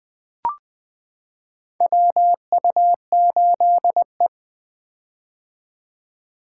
Morse code, heard as a keyed tone, about 20 wpm, 700 Hz. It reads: WU8E
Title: 2023 POTA Hunter Call Signs - 20wpm